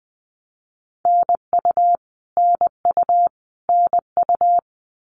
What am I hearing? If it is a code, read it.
Morse code, heard as a keyed tone, about 20 wpm, 700 Hz. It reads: NU NU NU
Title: Canadian Provinces and Territories - Repeated 3x - 20wpm